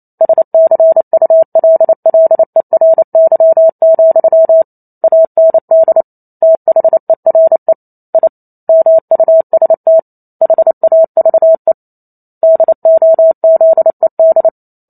Morse code, heard as a keyed tone, about 28 wpm, 650 Hz. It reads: SCULLERY, AND THERE I MUST HAVE DOZED